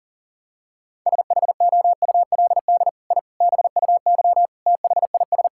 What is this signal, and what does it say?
Morse code, heard as a keyed tone, about 40 wpm, 690 Hz. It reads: SHOULD I BUY THIS
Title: Sentences from Top 1000 Words - Review Part 5 - 40wpm